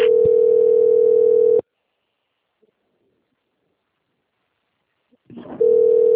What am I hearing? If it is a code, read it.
Ringback tone (North America), 440 + 480 Hz.